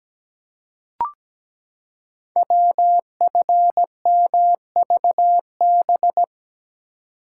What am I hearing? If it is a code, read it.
Morse code, heard as a keyed tone, about 17 wpm, 700 Hz. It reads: WFMVB
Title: Code Groups - 5 Letters - 17wpm